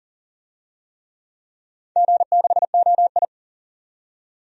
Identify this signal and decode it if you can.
Morse code, heard as a keyed tone, about 39 wpm, 700 Hz. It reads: G6OI
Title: DXCC-100 Call Signs - 4 Characters - 40wpm